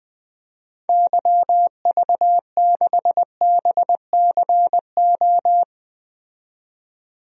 Morse code, heard as a keyed tone, about 20 wpm, 700 Hz. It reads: YV6BCO
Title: DXCC-100 Call Signs - 6 Characters - 20wpm